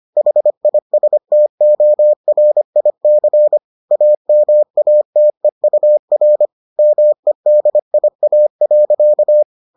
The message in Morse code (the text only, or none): HISTORIC AMATEUR MEDIA.